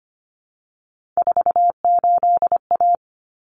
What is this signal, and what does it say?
Morse code, heard as a keyed tone, about 25 wpm, 700 Hz. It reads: 48A
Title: Common Cut Numbers - 3-Digit 25wpm